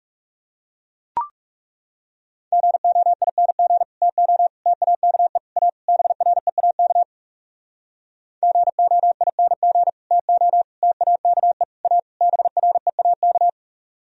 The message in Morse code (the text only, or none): GOING TO TAKE A BREAK GOING TO TAKE A BREAK